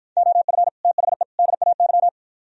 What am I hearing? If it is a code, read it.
Morse code, heard as a keyed tone, about 52 wpm, 700 Hz. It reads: OF THE DAY